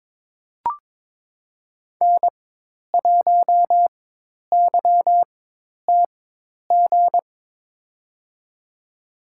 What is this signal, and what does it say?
Morse code, heard as a keyed tone, about 21 wpm, spaced as for 10 wpm, 700 Hz. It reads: N1YTG
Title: USA Call Signs - 5 Characters - 4x Character Spacing - 22wpm